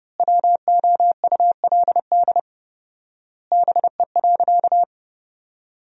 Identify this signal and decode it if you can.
Morse code, heard as a keyed tone, about 30 wpm, 700 Hz. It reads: WOULD BE.
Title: 4x Word-Spacing - Sentences from Top 200 Words - 30wpm